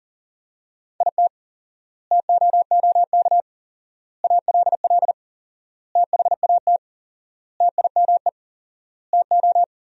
Morse code, heard as a keyed tone, about 37 wpm, 700 Hz. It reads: IT TOOK ALL THAT TIME TO
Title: 4x Word-Spacing - Sentences from Top 300 Words - 40wpm